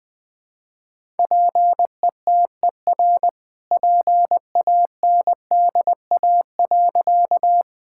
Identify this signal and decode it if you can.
Morse code, heard as a keyed tone, about 21 wpm, 700 Hz. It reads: PETER PANDA.
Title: All-Age Appropriate Jokes - 20wpm